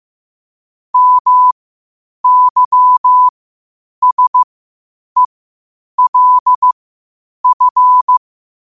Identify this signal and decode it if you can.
Morse code, heard as a keyed tone, about 15 wpm, spaced as for 8 wpm, 1000 Hz. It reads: MYSELF
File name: qotdfiles / 8wpm